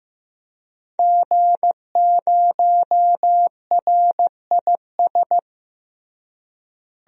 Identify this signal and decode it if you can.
Morse code, heard as a keyed tone, about 16 wpm, 700 Hz. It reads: G0RIS
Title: DXCC-100 Call Signs - 5 Characters - Repeated 3x - 15wpm